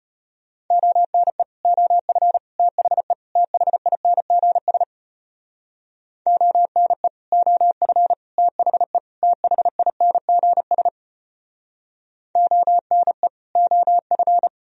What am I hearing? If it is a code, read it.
Morse code, heard as a keyed tone, about 34 wpm, 700 Hz. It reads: ONE OF THE THINGS ONE OF THE THINGS ONE OF